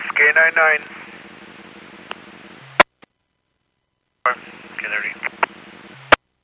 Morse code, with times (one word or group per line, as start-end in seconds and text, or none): none